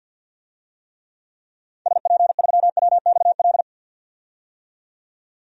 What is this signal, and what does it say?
Morse code, heard as a keyed tone, about 52 wpm, 700 Hz. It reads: SP3WXL
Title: DXCC-100 Call Signs - 6 Characters - 50wpm